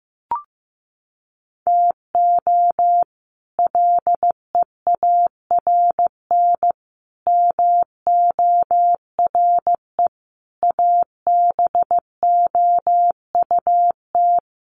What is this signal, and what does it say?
Morse code, heard as a keyed tone, about 15 wpm, 700 Hz. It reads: TO LEARN MORE ABOUT
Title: Sets of 5 Words Encore2 15wpm